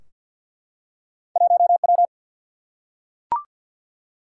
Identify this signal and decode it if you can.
Morse code, heard as a keyed tone, about 47 wpm, 700 Hz. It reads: 1 W
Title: Single-Letter Single-Number 50wpm